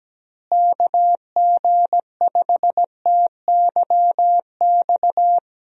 Morse code, heard as a keyed tone, about 17 wpm, 700 Hz. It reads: KG5TYX
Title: USA Call Signs - 6 Characters - Repeated 3x - 17wpm